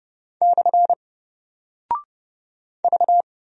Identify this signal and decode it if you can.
Morse code, heard as a keyed tone, about 30 wpm, 700 Hz. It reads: / V
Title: Single Character (Letter:Number:Punctuation:Prosign) Alt 30wpm